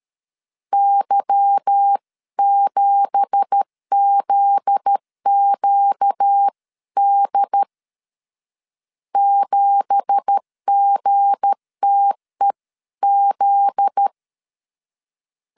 Morse code, heard as a keyed tone, about 13 wpm, 800 Hz. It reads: Y7ZQD 7GTEZ
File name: LEZIONI / 9 2867 12wpm 2322